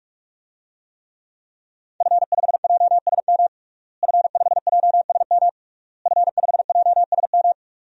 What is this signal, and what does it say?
Morse code, heard as a keyed tone, about 46 wpm, 700 Hz. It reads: F5JSM F5JSM F5JSM
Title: DXCC-100 Call Signs - 5 Characters - Repeated 3x - 45wpm